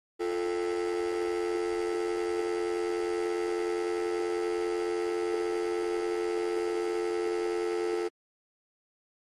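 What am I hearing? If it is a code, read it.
Dial tone (North America), 350 + 440 Hz, with noise.
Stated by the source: Telephone dial tone sound effect from the Sneak On The Lot library.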